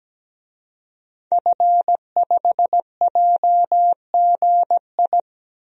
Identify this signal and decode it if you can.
Morse code, heard as a keyed tone, about 17 wpm, 700 Hz. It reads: F5JGI
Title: DXCC-100 Call Signs - 5 Characters - Repeated 2x - 17wpm